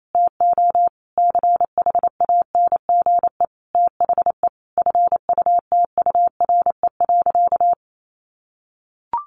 Morse code, heard as a keyed tone, about 28 wpm, 700 Hz. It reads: TO CHANGE THE FUTURE.
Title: A Sound Of Thunder 28wpm